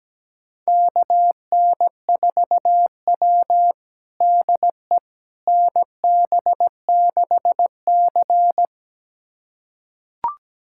Morse code, heard as a keyed tone, about 17 wpm, 700 Hz. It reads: KN4W DE NB6C
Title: USA Callsign DE Callsign - 4 Characters - 17wpm